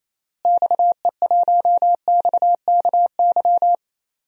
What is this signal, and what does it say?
Morse code, heard as a keyed tone, about 28 wpm, 700 Hz. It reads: XE1XKY
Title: DXCC-100 Call Signs - 6 Characters - Repeated 3x - 28wpm